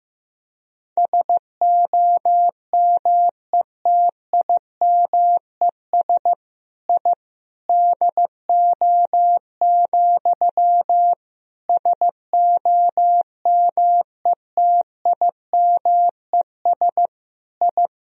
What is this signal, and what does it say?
Morse code, heard as a keyed tone, about 16 wpm, 700 Hz. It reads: SOMETIMES I DO, SOMETIMES I
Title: Sentences from Top 1000 Words - Review Part 3 - 15wpm